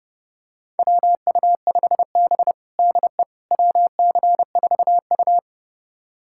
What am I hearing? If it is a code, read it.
Morse code, heard as a keyed tone, about 32 wpm, 700 Hz. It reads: WU5B DE WC4U
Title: USA Callsign DE Callsign - 4 Characters - 30wpm